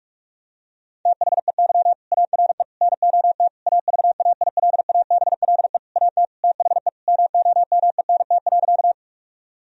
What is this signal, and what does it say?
Morse code, heard as a keyed tone, about 46 wpm, 700 Hz. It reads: THEY ARE NOT AVAILABLE AT THE MOMENT.
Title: Sentences from Top 1000 Words - Review Part 3 - 45wpm